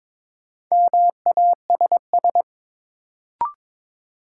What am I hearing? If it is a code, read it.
Morse code, heard as a keyed tone, about 22 wpm, 700 Hz. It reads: MASS
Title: Sets of 5 Words Encore - New Word Drill - 22wpm